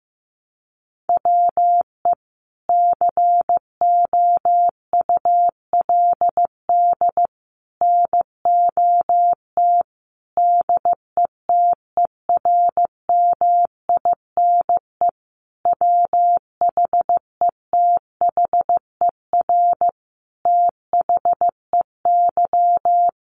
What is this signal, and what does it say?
Morse code, heard as a keyed tone, about 14 wpm, 700 Hz. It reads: WE COULD NOT DETERMINE WHETHER THEY
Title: Sentences from Top 1300 Words 15wpm